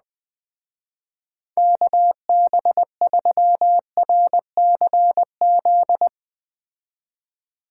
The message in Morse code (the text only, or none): KB3RCZ